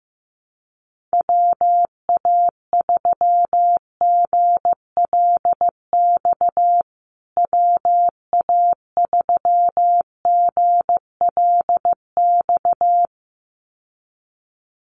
Morse code, heard as a keyed tone, about 14 wpm, 700 Hz. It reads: WA3GLX WA3GLX
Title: USA Call Signs Repeated 2x 15wpm